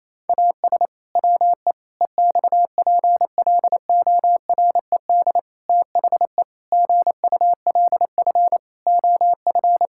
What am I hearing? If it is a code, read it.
Morse code, heard as a keyed tone, about 28 wpm, 700 Hz. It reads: AS WE EXPLORED THE GULF OF
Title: Pangram Challenge v2 - 28wpm